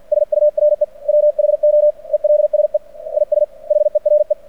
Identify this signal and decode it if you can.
Morse code, heard as a keyed tone, about 50 wpm, 600 Hz. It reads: SURE WHO ELSE IS HERE
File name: archived / CRSnet